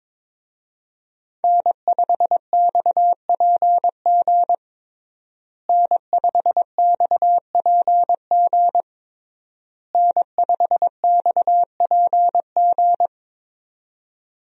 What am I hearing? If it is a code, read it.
Morse code, heard as a keyed tone, about 22 wpm, 700 Hz. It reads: N5XPG N5XPG N5XPG
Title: USA Call Signs - 5 Characters - Repeated 3x - 22wpm